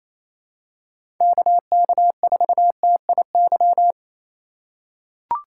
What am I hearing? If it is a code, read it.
Morse code, heard as a keyed tone, about 28 wpm, 700 Hz. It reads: KK4TIY
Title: USA Call Signs - 6 Characters - 2x Character Spacing - 28wpm